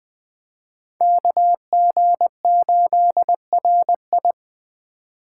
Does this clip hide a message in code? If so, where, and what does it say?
Yes: Morse code from 1.0 to 4.3 s, heard as a keyed tone, about 21 wpm, 700 Hz. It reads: KG8RI